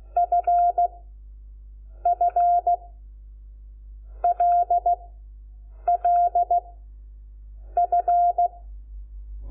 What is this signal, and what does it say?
Morse code, heard as a keyed tone, about 16 wpm, 700 Hz. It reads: F F L L F
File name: Lecciones / CLASE Nº5